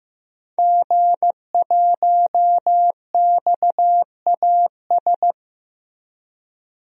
Morse code, heard as a keyed tone, about 15 wpm, 700 Hz. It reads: G1XAS
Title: DXCC-100 Call Signs - 5 Characters - Repeated 3x - 15wpm